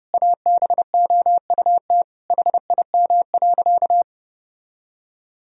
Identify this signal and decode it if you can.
Morse code, heard as a keyed tone, about 32 wpm, 700 Hz. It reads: ABOUT HIM.
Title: Sentences from Top 1000 Words - Review Part 1 - 30wpm